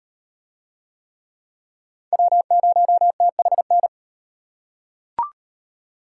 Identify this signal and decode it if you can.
Morse code, heard as a keyed tone, about 36 wpm, 700 Hz. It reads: W0THN